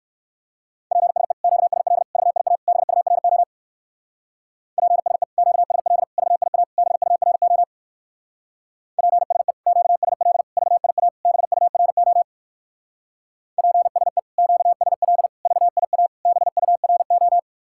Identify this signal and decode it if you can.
Morse code, heard as a keyed tone, about 58 wpm, 700 Hz. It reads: PSE QSL VIA BURO PSE QSL VIA BURO PSE QSL VIA BURO PSE QSL VIA BURO